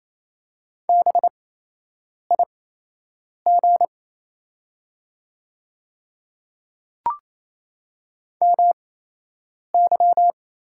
Morse code, heard as a keyed tone, about 27 wpm, spaced as for 8 wpm, 690 Hz. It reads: BIG MY
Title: 8x Character Spacing - Top 300 Words - 28wpm